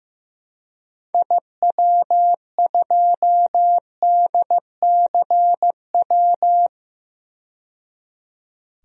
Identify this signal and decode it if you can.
Morse code, heard as a keyed tone, about 15 wpm, 700 Hz. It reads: IW2DCW
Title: DXCC-100 Call Signs - 6 Characters - 15wpm